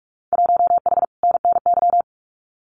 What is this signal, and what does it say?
Morse code, heard as a keyed tone, about 44 wpm, 700 Hz. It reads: 1H NNY
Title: Winter Field Day Exchanges 45wpm